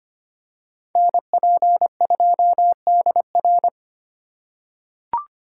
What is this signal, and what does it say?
Morse code, heard as a keyed tone, about 25 wpm, 700 Hz. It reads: NP2DR
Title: USA Call Signs - 5 Characters - 3x Character Spacing - 25wpm